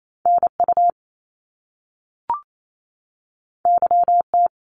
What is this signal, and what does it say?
Morse code, heard as a keyed tone, about 28 wpm, 700 Hz. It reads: NU YT
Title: Canadian Provinces and Territories 28wpm